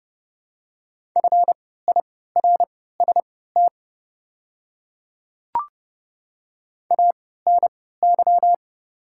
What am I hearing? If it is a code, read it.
Morse code, heard as a keyed tone, about 29 wpm, spaced as for 17 wpm, 690 Hz. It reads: FIRST ANY
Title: 3x Character Spacing - Top 200 Words - 30wpm